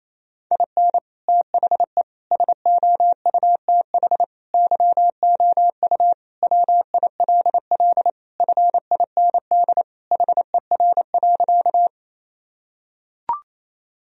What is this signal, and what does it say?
Morse code, heard as a keyed tone, about 28 wpm, 700 Hz. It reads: IN THE SOUTH YOU WILL FIND HER.
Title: Sentences from Top 1000 Words - Review Part 2 - 28wpm